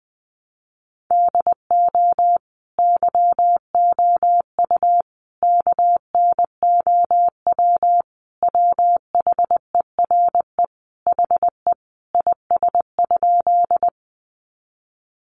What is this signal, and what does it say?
Morse code, heard as a keyed tone, about 20 wpm, 700 Hz. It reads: DO YOU KNOW WHERE HE IS?
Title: Sentences from Top 300 Words 20wpm